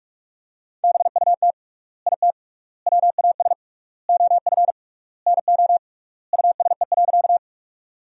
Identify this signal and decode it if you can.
Morse code, heard as a keyed tone, about 46 wpm, 700 Hz. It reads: BUT IT WAS OF NO USE.
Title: 3x Word-Spacing - Sentences from Top 100 Words - 45wpm